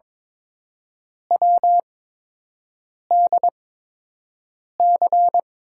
Morse code, heard as a keyed tone, about 22 wpm, spaced as for 6 wpm, 700 Hz. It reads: WDC
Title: USA Call Signs - 6 Characters - 8x Character Spacing - 22wpm